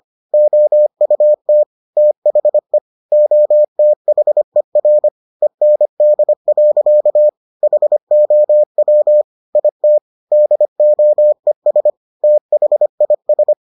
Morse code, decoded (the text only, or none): OUT THE OTHER END. HOW IT DOES THIS